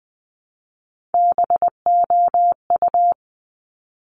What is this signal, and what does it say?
Morse code, heard as a keyed tone, about 20 wpm, 700 Hz. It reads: BOU
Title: Top Three Letter Combinations in Words 20wpm